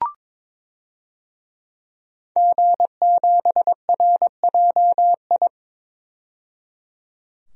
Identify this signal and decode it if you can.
Morse code, heard as a keyed tone, about 22 wpm, 700 Hz. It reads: G7RJI
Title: DXCC-100 Call Signs - 5 Characters - 22wpm